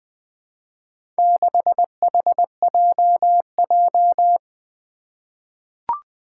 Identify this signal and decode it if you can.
Morse code, heard as a keyed tone, about 20 wpm, 700 Hz. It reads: 6HJJ
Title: Four Challenging Letter-Number Combinations 20wpm